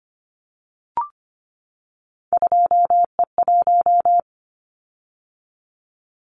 Morse code, heard as a keyed tone, about 25 wpm, 700 Hz. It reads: 2E1
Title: Callsign Prefixes for Top 25 DXCC - 25wpm